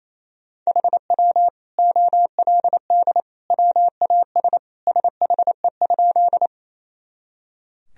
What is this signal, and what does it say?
Morse code, heard as a keyed tone, about 27 wpm, 700 Hz. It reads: HW OLD WAS SHE?
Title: Sentences from Top 200 Words and Abbreviations 28wpm